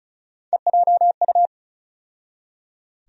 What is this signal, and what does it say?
Morse code, heard as a keyed tone, about 36 wpm, 700 Hz. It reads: EJU
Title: USA Call Sign Postfixes - 3 Letters 35wpm